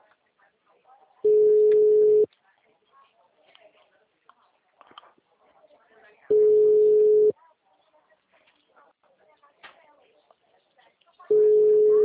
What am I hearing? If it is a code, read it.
Ringback tone (Europe, 425 Hz), with speech.